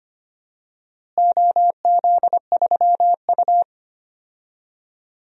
Morse code, heard as a keyed tone, about 25 wpm, 700 Hz. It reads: OZ3U
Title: DXCC-100 Call Signs - 4 Characters - Repeated 3x - 25wpm